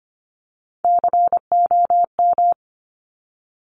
Morse code, heard as a keyed tone, about 25 wpm, 700 Hz. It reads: COM
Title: Top Three Letter Combinations in Words 25wpm